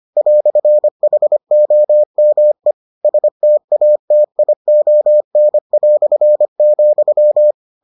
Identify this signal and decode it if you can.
Morse code, heard as a keyed tone, about 25 wpm, 600 Hz. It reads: "HOME STATION",